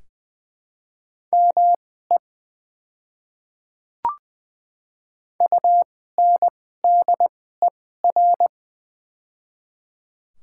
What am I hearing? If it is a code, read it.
Morse code, heard as a keyed tone, about 21 wpm, spaced as for 15 wpm, 700 Hz. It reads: ME UNDER
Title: 2x Character Spacing - Top 200 Words - 20wpm